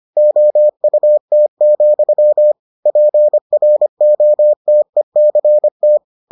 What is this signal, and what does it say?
Morse code, heard as a keyed tone, about 25 wpm, 600 Hz. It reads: OUT, PROTECT